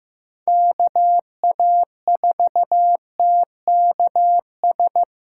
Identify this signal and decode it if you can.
Morse code, heard as a keyed tone, about 15 wpm, 700 Hz. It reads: KA4TKS
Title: USA Call Signs - 6 Characters - 4x Character Spacing - 15wpm